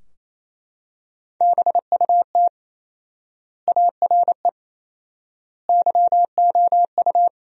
Morse code, heard as a keyed tone, about 28 wpm, 700 Hz. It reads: BUT ARE YOU
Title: 4x Word-Spacing - Sentences from Top 300 Words - 28wpm